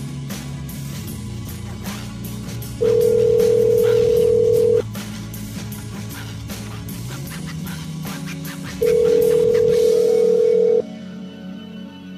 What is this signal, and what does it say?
Ringback tone (North America), 440 + 480 Hz.